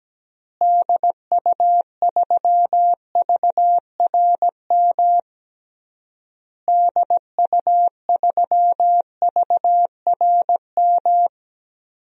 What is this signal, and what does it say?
Morse code, heard as a keyed tone, about 17 wpm, 700 Hz. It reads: DU3VRM DU3VRM
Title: DXCC-100 Call Signs - 6 Characters - Repeated 3x - 17wpm